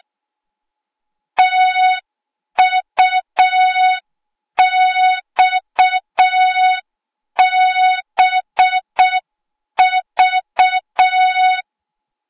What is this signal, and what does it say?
Morse code, heard as a keyed tone, about 6 wpm, 3700 Hz. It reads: TUXBV
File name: Morze-HA2MN / hang / 08KXC HA2MN